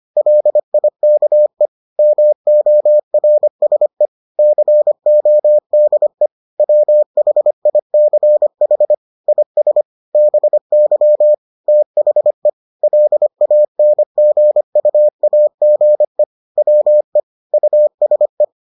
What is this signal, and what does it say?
Morse code, heard as a keyed tone, about 25 wpm, 600 Hz. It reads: LIKE MORSE CODE WHICH IS BY THE LANGUAGE WE USE